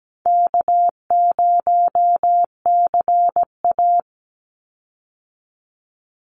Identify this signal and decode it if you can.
Morse code, heard as a keyed tone, about 17 wpm, 700 Hz. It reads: K0CA
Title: Easy Call Signs - 17wpm